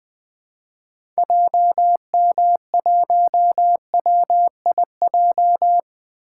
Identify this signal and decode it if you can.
Morse code, heard as a keyed tone, about 20 wpm, 700 Hz. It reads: JM1WIJ
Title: 2023 POTA Activator Call Signs - 20wpm